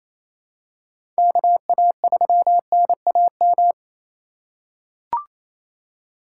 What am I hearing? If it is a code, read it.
Morse code, heard as a keyed tone, about 28 wpm, 700 Hz. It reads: KA3NAM
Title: K1USN SST QSO Practice - 28wpm